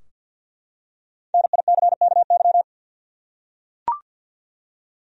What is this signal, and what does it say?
Morse code, heard as a keyed tone, about 51 wpm, 700 Hz. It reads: NICKY
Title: USA Common Names - Max 5 Letters - 50wpm